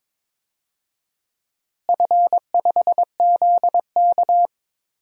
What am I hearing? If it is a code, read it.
Morse code, heard as a keyed tone, about 22 wpm, 700 Hz. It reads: F5ZK
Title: DXCC-100 Call Signs - 4 Characters - Repeated 2x - 22wpm